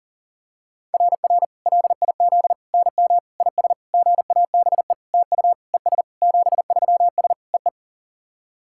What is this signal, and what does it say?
Morse code, heard as a keyed tone, about 40 wpm, 690 Hz. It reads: RR LIZ NM IS GABE TU ES 73S EE
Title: SOTA QSO Practice 40wpm